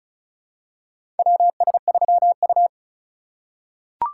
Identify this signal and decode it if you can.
Morse code, heard as a keyed tone, about 34 wpm, 700 Hz. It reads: WS3U
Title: USA Call Signs - 4 Characters - 2x Character Spacing - 35wpm